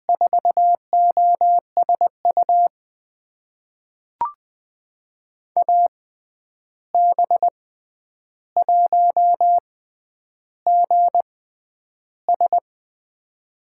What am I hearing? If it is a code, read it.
Morse code, heard as a keyed tone, about 20 wpm, spaced as for 10 wpm, 700 Hz. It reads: 4OSU AB1GS
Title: USA Call Signs - 5 Characters - 6x Character Spacing - 20wpm